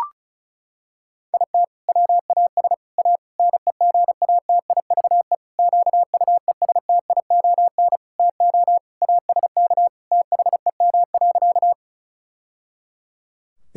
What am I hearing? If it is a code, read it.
Morse code, heard as a keyed tone, about 34 wpm, 700 Hz. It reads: IT WAS A NEGATIVE QUESTION TO ASK THEM.
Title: Sentences from Top 1000 Words - Review Part 6 - 35wpm